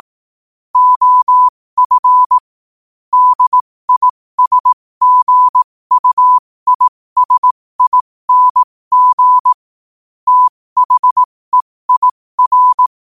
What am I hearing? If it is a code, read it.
Morse code, heard as a keyed tone, about 18 wpm, spaced as for 15 wpm, 1000 Hz. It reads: OF DISGUISING THEIR
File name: qotdfiles / 15wpm